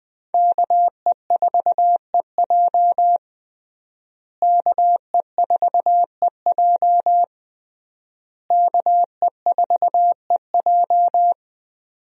Morse code, heard as a keyed tone, about 21 wpm, 700 Hz. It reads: KE4EJ KE4EJ KE4EJ
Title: USA Call Signs - 5 Characters - Repeated 3x - 20wpm